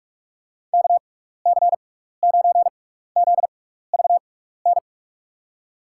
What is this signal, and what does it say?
Morse code, heard as a keyed tone, about 44 wpm, spaced as for 15 wpm, 700 Hz. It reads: KC9ZVN
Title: USA Call Signs - 6x Character Spacing - 45wpm